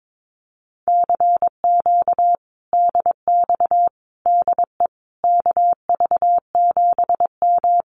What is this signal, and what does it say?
Morse code, heard as a keyed tone, about 22 wpm, 700 Hz. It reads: CQ DX DE KV7M
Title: Common QSO Phrases Encore 22wpm